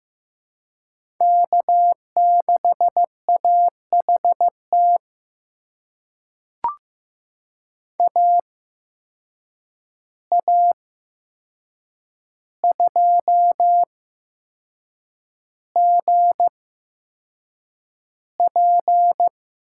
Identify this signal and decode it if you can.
Morse code, heard as a keyed tone, about 15 wpm, spaced as for 6 wpm, 700 Hz. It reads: K6AHT AA2GP
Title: USA Call Signs - 5 Characters - 8x Character Spacing - 15wpm